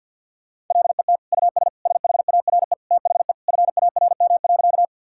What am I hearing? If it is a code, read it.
Morse code, heard as a keyed tone, about 50 wpm, 700 Hz. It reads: LET US SHARE THE FARM.
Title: Sentences from Top 1000 Words - Review Part 6 - 50wpm